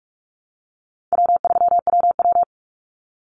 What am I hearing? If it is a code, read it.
Morse code, heard as a keyed tone, about 44 wpm, 700 Hz. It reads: W3WW
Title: Call Signs Repeated 2x - 45wpm